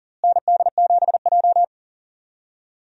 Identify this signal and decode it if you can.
Morse code, heard as a keyed tone, about 40 wpm, 700 Hz. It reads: ND7J
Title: USA Call Signs - 4 Characters - Repeated 3x - 40wpm